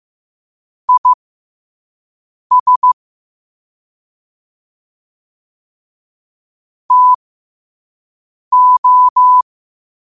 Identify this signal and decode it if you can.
Morse code, heard as a keyed tone, about 15 wpm, spaced as for 5 wpm, 1000 Hz. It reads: IS TO
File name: qotdfiles / 5wpm